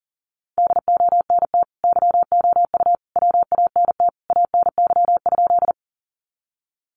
Morse code, heard as a keyed tone, about 37 wpm, 700 Hz. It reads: DONT YOU WANT ANY?
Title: Sentences from Top 300 Words 40wpm